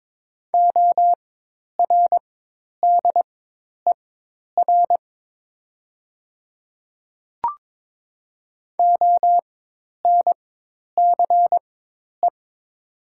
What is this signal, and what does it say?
Morse code, heard as a keyed tone, about 22 wpm, spaced as for 10 wpm, 700 Hz. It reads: ORDER ONCE
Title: 4x Character Spacing - Top 500 Words - 22wpm